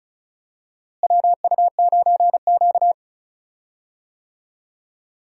Morse code, heard as a keyed tone, about 36 wpm, 700 Hz. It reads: WU9Q
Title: USA Call Signs - 4 Characters - Repeated 2x - 35wpm